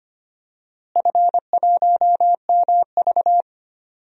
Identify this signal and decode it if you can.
Morse code, heard as a keyed tone, about 25 wpm, 700 Hz. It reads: F1MV
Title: DXCC-100 Call Signs - 4 Characters - Repeated 2x - 25wpm